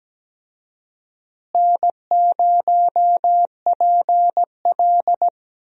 Morse code, heard as a keyed tone, about 17 wpm, 700 Hz. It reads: N0PL
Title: USA Call Signs - 4 Characters - Repeated 2x - 17wpm